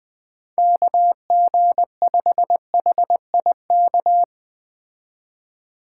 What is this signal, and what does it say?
Morse code, heard as a keyed tone, about 20 wpm, 700 Hz. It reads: KG5HIK
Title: Speed Racing - USA Call Signs 17wpm